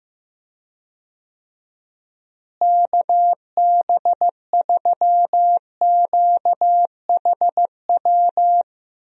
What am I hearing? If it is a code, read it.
Morse code, heard as a keyed tone, about 15 wpm, 700 Hz. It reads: KB3QHW
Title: USA Call Signs - 6 Characters - Repeated 2x - 15wpm